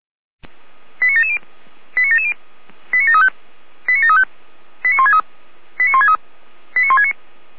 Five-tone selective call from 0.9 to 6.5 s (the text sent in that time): 89770 89770 89735 89735 89163 89163